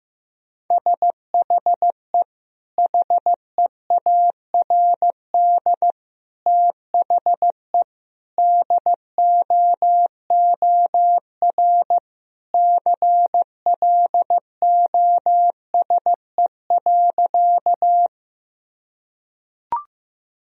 Morse code, heard as a keyed tone, about 15 wpm, 700 Hz. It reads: SHE HEARD THE DOOR CLOSE.
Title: Sentences from Top 1000 Words - Review Part 1 - 15wpm